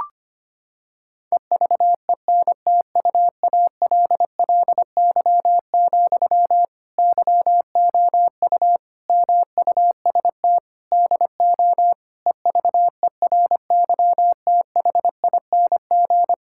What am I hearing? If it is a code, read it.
Morse code, heard as a keyed tone, about 25 wpm, 700 Hz. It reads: EVENTUALLY, YOU MUST DO EVERYTHING
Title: Sentences from Top 1000 Words - Review Part 6 - 25wpm